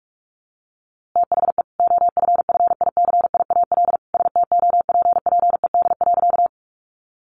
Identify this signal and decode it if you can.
Morse code, heard as a keyed tone, about 45 wpm, 700 Hz. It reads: THE OFFICIAL STOPPED.
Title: A Sound Of Thunder 45wpm